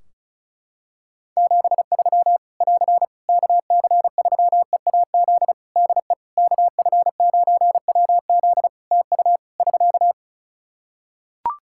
Morse code, heard as a keyed tone, about 34 wpm, 700 Hz. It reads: 73 <AR> KC3EAZ DE KF9WZ TU <SK>
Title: Practice QSO - Exchange 4 - Ending - 35wpm